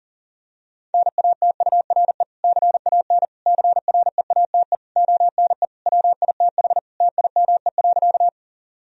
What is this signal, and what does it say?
Morse code, heard as a keyed tone, about 40 wpm, 700 Hz. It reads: NATURE CAN CREATE ONE WITH TIME.
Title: Sentences from Top 1000 Words - Review Part 3 - 40wpm